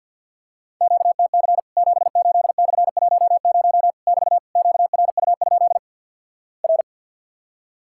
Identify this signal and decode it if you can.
Morse code, heard as a keyed tone, about 50 wpm, 700 Hz. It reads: QTC 78/10 <BT> QRU? R
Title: Worked All Europe DX Contest - QTC Exchange - 50wpm